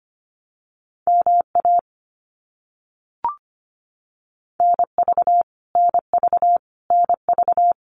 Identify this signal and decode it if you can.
Morse code, heard as a keyed tone, about 25 wpm, 700 Hz. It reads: MA NV NV NV
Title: US States Repeated 3x 25wpm